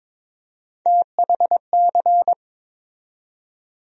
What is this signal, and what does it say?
Morse code, heard as a keyed tone, about 23 wpm, 700 Hz. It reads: THC
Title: USA Call Sign Postfixes - 3 Letters 22wpm